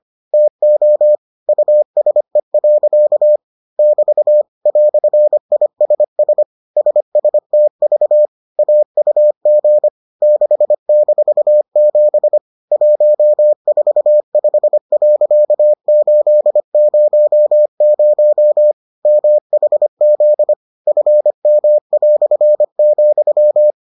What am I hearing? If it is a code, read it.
Morse code, heard as a keyed tone, about 25 wpm, 600 Hz. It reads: TO USE. <BT> "ISS SSTV AUG 6-7 145.800 MHZ FM",